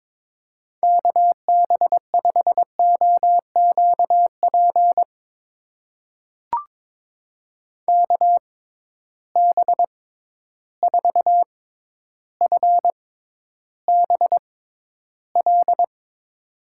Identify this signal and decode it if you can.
Morse code, heard as a keyed tone, about 22 wpm, spaced as for 11 wpm, 700 Hz. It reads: KB5OQP KB4FBL
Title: USA Call Signs - 6 Characters - 6x Character Spacing - 22wpm